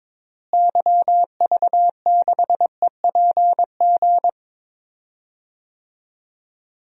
Morse code, heard as a keyed tone, about 22 wpm, 700 Hz. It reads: YV6EPG
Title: DXCC-100 Call Signs - 6 Characters - Repeated 2x - 22wpm